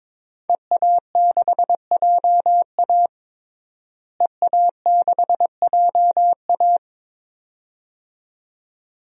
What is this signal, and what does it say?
Morse code, heard as a keyed tone, about 22 wpm, 700 Hz. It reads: EA6JA EA6JA
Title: DXCC-100 Call Signs - 5 Characters - Repeated 2x - 22wpm